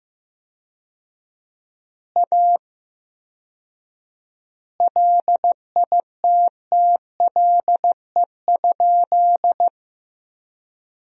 Morse code, heard as a keyed tone, about 15 wpm, 700 Hz. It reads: A LITTLE?
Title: 4x Word-Spacing - Sentences from Top 500 Words - 15wpm